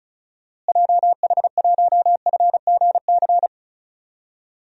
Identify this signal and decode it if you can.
Morse code, heard as a keyed tone, about 34 wpm, 700 Hz. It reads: JH1FGC
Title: DXCC-100 Call Signs - 6 Characters - Repeated 3x - 35wpm